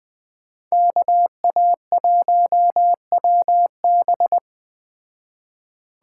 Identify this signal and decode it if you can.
Morse code, heard as a keyed tone, about 20 wpm, 700 Hz. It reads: KA1WB
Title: Speed Racing - USA 2x2 Call Signs 17wpm